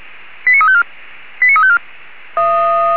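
Five-tone selective call (ZVEI), heard as quick runs of five tones, tones of about 70 ms. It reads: 89354 89354
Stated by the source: ZVEI five-tone selective call 89346 received via SDR.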